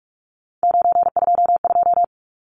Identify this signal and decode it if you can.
Morse code, heard as a keyed tone, about 44 wpm, 700 Hz. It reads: 922
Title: Numbers - 3-Digit 45wpm